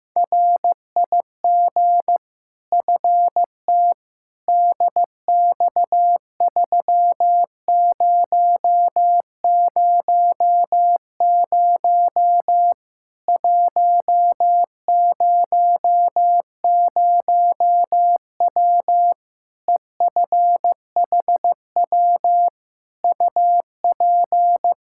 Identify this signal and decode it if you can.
Morse code, heard as a keyed tone, about 15 wpm, 700 Hz. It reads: RIG FT DX3000 100W EFHW UP
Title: Practice QSO - Exchange 2 - Info - 15wpm